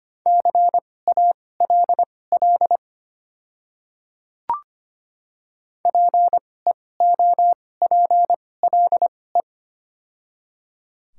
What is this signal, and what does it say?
Morse code, heard as a keyed tone, about 25 wpm, spaced as for 18 wpm, 700 Hz. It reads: CALL PEOPLE
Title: 2x Character Spacing - Top 100 Words - 25wpm